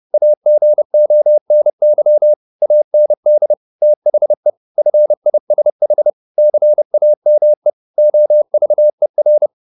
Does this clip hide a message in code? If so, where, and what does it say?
Yes: Morse code from 0.1 to 9.5 s, heard as a keyed tone, about 30 wpm, 600 Hz. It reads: AGONY AND THE FISH CAME OVER